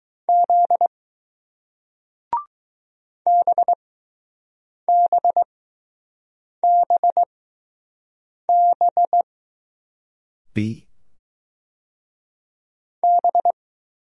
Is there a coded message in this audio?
Yes: Morse code from 0.3 to 13.5 s, heard as a keyed tone, about 20 wpm, spaced as for 6 wpm, 700 Hz. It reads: Z BBBB B